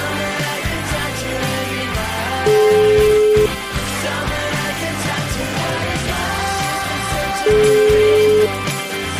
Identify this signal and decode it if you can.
Ringback tone (Europe, 425 Hz), with noise.